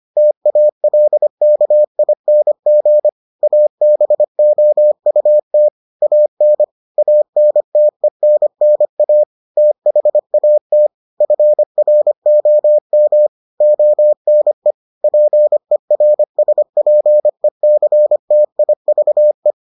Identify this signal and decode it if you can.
Morse code, heard as a keyed tone, about 25 wpm, 600 Hz. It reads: TALKING ABOUT AN ANTENNA THAT FROM ONE PERSPECTIVE